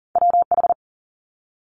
Morse code, heard as a keyed tone, about 41 wpm, 700 Hz. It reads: WH
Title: Multi-letter Phonemes Quick Fire 40wpm